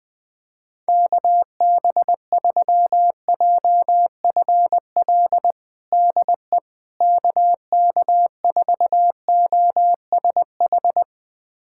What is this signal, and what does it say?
Morse code, heard as a keyed tone, about 20 wpm, 700 Hz. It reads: KB3JFL DE KK4OSH
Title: USA Callsign DE Callsign 20wpm